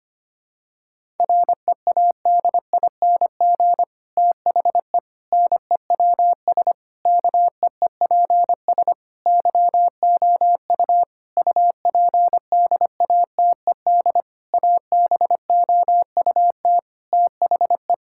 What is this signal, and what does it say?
Morse code, heard as a keyed tone, about 25 wpm, 700 Hz. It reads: READING THE NEWS KEEPS YOU UPDATED ABOUT THE
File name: practice / 25wpm / Difficult Sentences with Hints - 25wpm